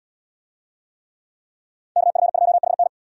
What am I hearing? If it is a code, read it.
Morse code, heard as a keyed tone, about 68 wpm, 700 Hz. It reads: KF2HN